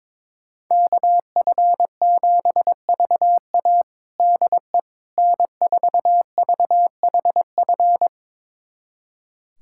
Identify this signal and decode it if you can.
Morse code, heard as a keyed tone, about 22 wpm, 700 Hz. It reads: KF7VA DE N4VHF
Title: USA Callsign DE Callsign - 5 Characters - 22wpm